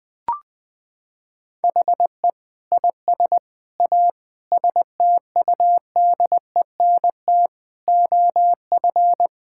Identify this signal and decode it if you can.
Morse code, heard as a keyed tone, about 20 wpm, 700 Hz. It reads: HE IS A STUDENT OF
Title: Sentences from Top 1000 Words - Review Part 6 - 20wpm